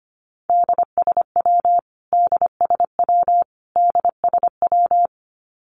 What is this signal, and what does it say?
Morse code, heard as a keyed tone, about 25 wpm, 700 Hz. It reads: DSW DSW DSW
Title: Abbreviations Repeated 3x - 25wpm